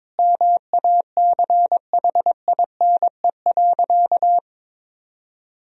Morse code, heard as a keyed tone, about 22 wpm, 700 Hz. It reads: MACHINE.
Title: Sentences from Top 1000 Words - Review Part 3 - 22wpm